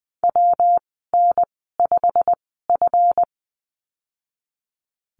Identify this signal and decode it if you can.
Morse code, heard as a keyed tone, about 20 wpm, spaced as for 15 wpm, 700 Hz. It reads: WN5F
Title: 193 Lesson 31g - Letter v - Call Signs 20wpm